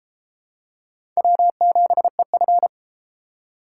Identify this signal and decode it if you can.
Morse code, heard as a keyed tone, about 28 wpm, 700 Hz. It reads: W7EF